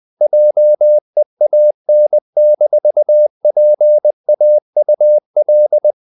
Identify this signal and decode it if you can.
Morse code, heard as a keyed tone, about 20 wpm, 600 Hz. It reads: JEAN-PAUL
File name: CW / Practice / EnglishQuotations0320